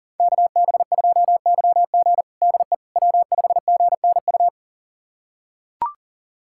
Morse code, heard as a keyed tone, about 40 wpm, 700 Hz. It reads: KB2YG DE W5GNU
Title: USA Callsign DE Callsign - 5 Characters - 40wpm